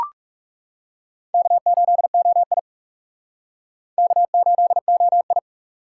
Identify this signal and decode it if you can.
Morse code, heard as a keyed tone, about 40 wpm, 700 Hz. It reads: K8OI K8OI